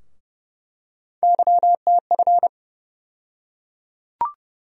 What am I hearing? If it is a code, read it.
Morse code, heard as a keyed tone, about 30 wpm, 700 Hz. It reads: YTF
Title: USA Call Sign Postfixes - 3 Letters 30wpm